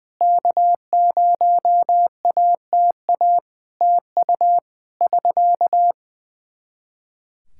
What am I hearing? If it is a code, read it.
Morse code, heard as a keyed tone, about 19 wpm, 700 Hz. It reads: K0ATA TU <SK>
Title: Practice QSO - Exchange 4 - Ending - 20wpm